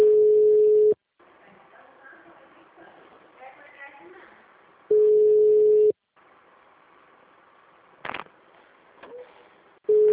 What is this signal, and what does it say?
Ringback tone (Europe, 425 Hz), with speech.